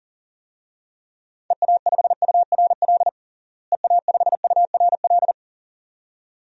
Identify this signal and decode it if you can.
Morse code, heard as a keyed tone, about 40 wpm, 700 Hz. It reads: EA5URL EA5URL
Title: DXCC-100 Call Signs - 6 Characters - Repeated 2x - 40wpm